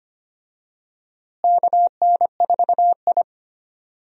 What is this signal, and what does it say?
Morse code, heard as a keyed tone, about 25 wpm, 700 Hz. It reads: KN4I
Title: USA Call Signs - 4 Characters - Repeated 3x - 25wpm